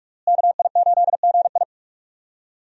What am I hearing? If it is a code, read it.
Morse code, heard as a keyed tone, about 43 wpm, 700 Hz. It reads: KI8GI
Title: USA Call Signs - 5 Characters - Repeated 2x - 45wpm